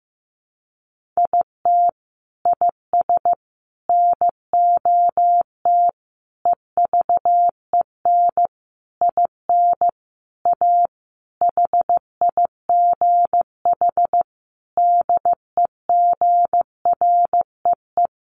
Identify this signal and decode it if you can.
Morse code, heard as a keyed tone, about 15 wpm, 700 Hz. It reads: IT IS NOT EVEN IN A HIGH DEGREE
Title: Sentences from Top 1500 Words 15wpm